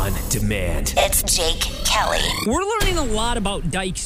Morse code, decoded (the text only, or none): none